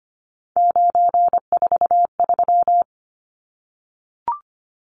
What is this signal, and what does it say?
Morse code, heard as a keyed tone, about 25 wpm, 700 Hz. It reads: 943
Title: Numbers - 3-Digit 25wpm